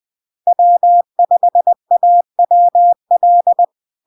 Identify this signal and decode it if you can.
Morse code, heard as a keyed tone, about 20 wpm, 700 Hz. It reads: W5AWL
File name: LARK-FD-2022-Log-in-CW-20wpm-700-Hz